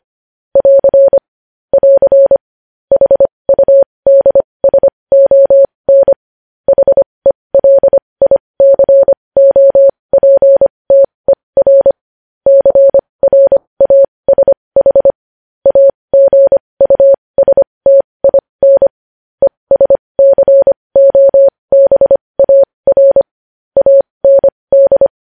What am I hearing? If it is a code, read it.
Morse code, heard as a keyed tone, about 25 wpm, spaced as for 20 wpm, 550 Hz. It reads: <AR> <AR> HUDSON HELICOPTER CRASH AGUSTIN ESCOBAR AND
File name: podcast / News headlines-1x-25x20